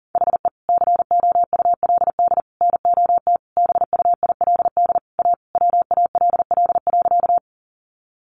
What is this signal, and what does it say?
Morse code, heard as a keyed tone, about 40 wpm, 700 Hz. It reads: HE COULD NOT BUILD A WALL.
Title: Sentences from Top 1100 Words 40wpm